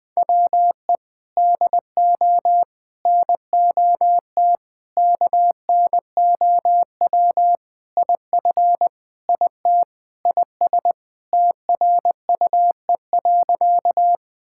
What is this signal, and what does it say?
Morse code, heard as a keyed tone, about 21 wpm, 700 Hz. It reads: WE DO NOT KNOW IF IT IS TRUE.
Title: Sentences with 5-Word Phrases - 20wpm